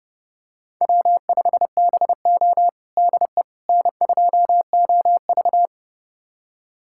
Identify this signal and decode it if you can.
Morse code, heard as a keyed tone, about 32 wpm, 700 Hz. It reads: W5BO DE N2OV
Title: USA Callsign DE Callsign - 4 Characters - 30wpm